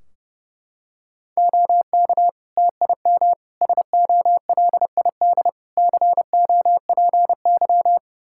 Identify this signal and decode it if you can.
Morse code, heard as a keyed tone, about 30 wpm, 700 Hz. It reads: OK TIM SOLID COPY
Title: Practice QSO - Exchange 3 - More Info - 30wpm